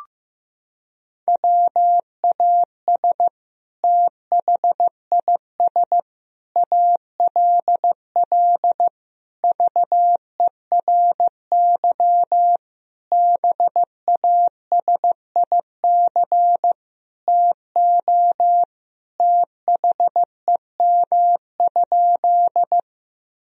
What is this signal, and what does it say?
Morse code, heard as a keyed tone, about 15 wpm, 700 Hz. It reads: WAS THIS ALL VERY BASIC TO THEM?
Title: Sentences from Top 1000 Words - Review Part 3 - 15wpm